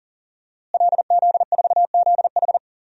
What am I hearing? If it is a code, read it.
Morse code, heard as a keyed tone, about 40 wpm, 700 Hz. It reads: LZ4ZH
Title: DXCC-100 Call Signs - 5 Characters - Repeated 3x - 40wpm